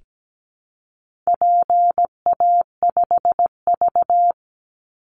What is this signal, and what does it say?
Morse code, heard as a keyed tone, about 17 wpm, 700 Hz. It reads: PA5V
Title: Call Signs Repeated 3x - 17wpm